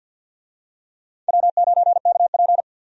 Morse code, heard as a keyed tone, about 52 wpm, 700 Hz. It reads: W9KP
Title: USA Call Signs - 4 Characters - 50wpm